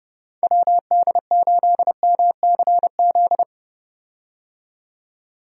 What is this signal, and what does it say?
Morse code, heard as a keyed tone, about 30 wpm, 700 Hz. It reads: WD8MCZ
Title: USA Call Signs - 6 Characters - 30wpm